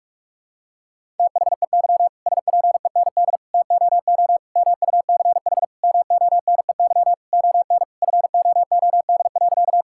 Morse code, heard as a keyed tone, about 41 wpm, 700 Hz. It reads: THEY SPEND TOO MUCH MONEY ON FOOD.
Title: Sentences from Top 1000 Words - Review Part 5 - 45wpm